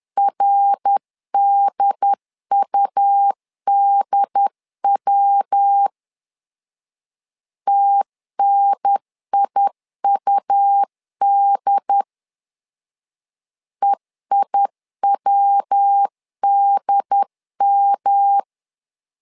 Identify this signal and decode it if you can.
Morse code, heard as a keyed tone, about 10 wpm, 800 Hz. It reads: RDUDW TNIUD EIWDM
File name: LEZIONI / 3 MWRO / 3 MWR 10wpm 1952